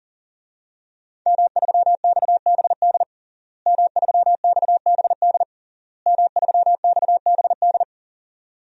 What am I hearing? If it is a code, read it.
Morse code, heard as a keyed tone, about 40 wpm, 700 Hz. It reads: M3XBD M3XBD M3XBD
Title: DXCC-100 Call Signs - 5 Characters - Repeated 3x - 40wpm